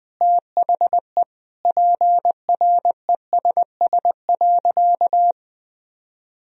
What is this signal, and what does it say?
Morse code, heard as a keyed tone, about 20 wpm, 700 Hz. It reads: THE PRESS.
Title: Sentences from Top 1000 Words - Review Part 1 - 20wpm